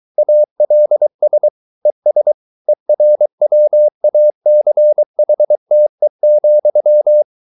Morse code, heard as a keyed tone, about 23 wpm, 600 Hz. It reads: ALS ES ERWACHTE,